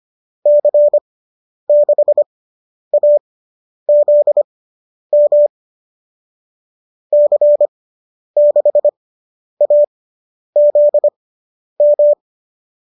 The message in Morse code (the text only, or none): C6AZM C6AZM